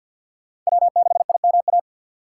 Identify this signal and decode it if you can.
Morse code, heard as a keyed tone, about 64 wpm, 700 Hz. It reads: W6IMA